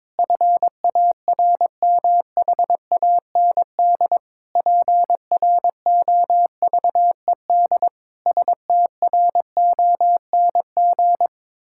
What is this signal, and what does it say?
Morse code, heard as a keyed tone, about 22 wpm, 700 Hz. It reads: FARMHAND PROVED STRONG
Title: Pangram Challenge v2 - 22wpm